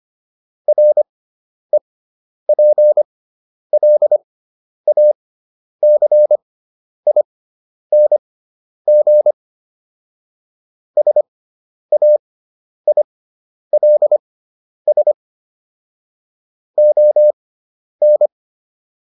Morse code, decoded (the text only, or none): REPLACING SAILS ON